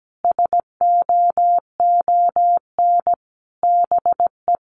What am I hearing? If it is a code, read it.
Morse code, heard as a keyed tone, about 17 wpm, 700 Hz. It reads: SOON BE
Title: Sentences from Top 1300 Words 17wpm